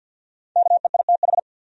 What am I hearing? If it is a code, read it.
Morse code, heard as a keyed tone, about 50 wpm, 700 Hz. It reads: KEITH
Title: USA Common Names - 50wpm